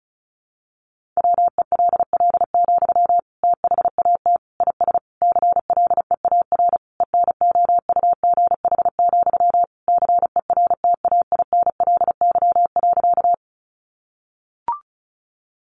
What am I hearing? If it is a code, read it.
Morse code, heard as a keyed tone, about 34 wpm, 700 Hz. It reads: WELL, THAT IS CLEAR ENOUGH, CERTAINLY.
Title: Sentences from Top 900 Words 35wpm